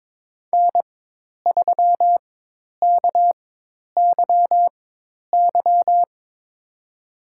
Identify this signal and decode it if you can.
Morse code, heard as a keyed tone, about 23 wpm, spaced as for 10 wpm, 700 Hz. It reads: N3KYY
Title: USA Call Signs - 4x Character Spacing - 22wpm